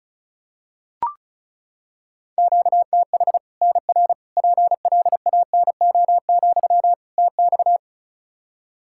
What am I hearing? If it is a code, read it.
Morse code, heard as a keyed tone, about 36 wpm, 700 Hz. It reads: QTH NR PLANO, TX
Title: Practice QSO - Exchange 1 - Beginning - 35wpm